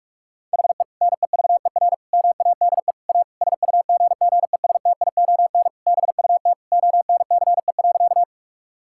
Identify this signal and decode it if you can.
Morse code, heard as a keyed tone, about 42 wpm, 700 Hz. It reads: HE NEVER MADE A SUGGESTION BUT ONCE.
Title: Sentences from Top 1600 Words 45wpm